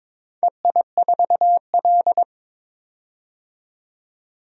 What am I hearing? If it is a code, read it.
Morse code, heard as a keyed tone, about 22 wpm, 700 Hz. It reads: EI4L
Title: DXCC-100 Call Signs - 4 Characters - 22wpm